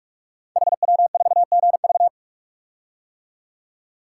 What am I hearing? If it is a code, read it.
Morse code, heard as a keyed tone, about 44 wpm, 700 Hz. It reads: HW4GV
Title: Code Groups - 5 Letter:Numbers - 45wpm